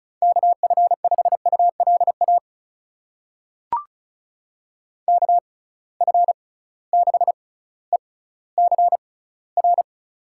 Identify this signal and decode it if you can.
Morse code, heard as a keyed tone, about 36 wpm, spaced as for 18 wpm, 700 Hz. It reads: KF5ULA KF6ECR
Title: USA Call Signs - 6 Characters - 6x Character Spacing - 35wpm